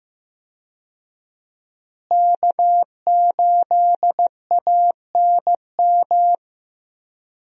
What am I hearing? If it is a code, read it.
Morse code, heard as a keyed tone, about 15 wpm, 700 Hz. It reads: K8ANM
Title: USA Call Signs - 5 Characters - Repeated 3x - 15wpm